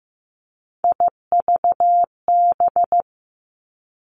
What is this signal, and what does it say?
Morse code, heard as a keyed tone, about 15 wpm, 700 Hz. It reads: IVB
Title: Three Challenging Letter-Number Combinations 15wpm